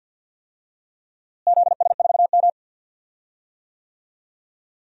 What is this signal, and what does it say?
Morse code, heard as a keyed tone, about 51 wpm, 700 Hz. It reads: ZS4M
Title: DXCC-100 Call Signs - 4 Characters - Repeated 2x - 50wpm